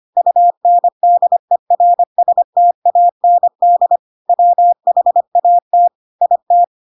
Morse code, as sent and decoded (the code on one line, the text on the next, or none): ..- -. -.. .-. ... - .- -. -.. / .-- .... .- - / .. -
UNDERSTAND WHAT IT